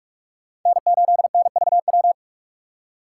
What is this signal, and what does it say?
Morse code, heard as a keyed tone, about 42 wpm, 700 Hz. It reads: N8NVW